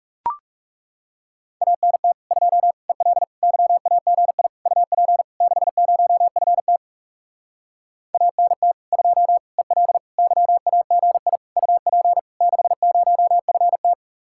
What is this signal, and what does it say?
Morse code, heard as a keyed tone, about 40 wpm, 700 Hz. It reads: ANT 2 EL YAGI UP 60FT ANT 2 EL YAGI UP 60FT